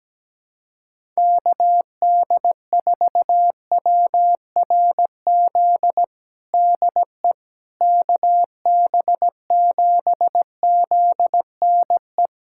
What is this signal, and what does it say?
Morse code, heard as a keyed tone, about 17 wpm, 700 Hz. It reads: KD4WRZ DE KB7ZNE
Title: USA Callsign DE Callsign - 6 Characters - 17wpm